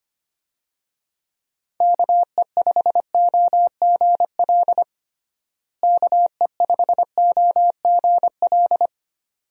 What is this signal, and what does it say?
Morse code, heard as a keyed tone, about 25 wpm, 700 Hz. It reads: KE5OGL KE5OGL
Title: USA Call Signs - 6 Characters - Repeated 2x - 25wpm